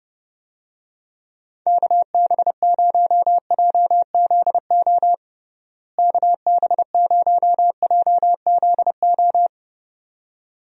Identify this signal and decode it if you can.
Morse code, heard as a keyed tone, about 30 wpm, 700 Hz. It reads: KB0JZO KB0JZO
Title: USA Call Signs - 6 Characters - Repeated 2x - 30wpm